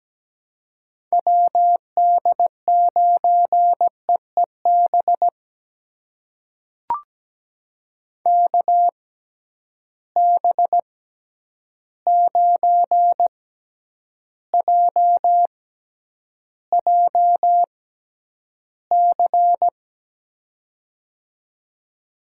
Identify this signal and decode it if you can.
Morse code, heard as a keyed tone, about 17 wpm, spaced as for 9 wpm, 700 Hz. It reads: WD9EEB KB9JJC
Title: USA Call Signs - 6 Characters - 6x Character Spacing - 17wpm